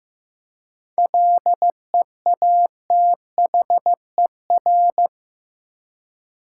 Morse code, heard as a keyed tone, about 15 wpm, 700 Hz. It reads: LEATHER
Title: Seven Letter Words - 15wpm